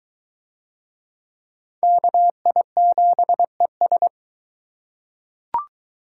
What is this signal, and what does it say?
Morse code, heard as a keyed tone, about 21 wpm, 700 Hz. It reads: KI7ES